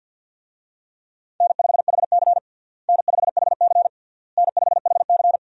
Morse code, heard as a keyed tone, about 53 wpm, 700 Hz. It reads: N5HC N5HC N5HC
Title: USA Call Signs - 4 Characters - Repeated 3x - 50wpm